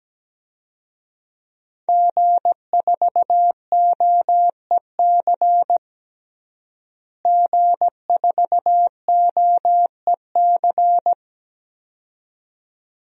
Morse code, heard as a keyed tone, about 17 wpm, 700 Hz. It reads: G4OEC G4OEC
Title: DXCC-100 Call Signs - 5 Characters - Repeated 2x - 17wpm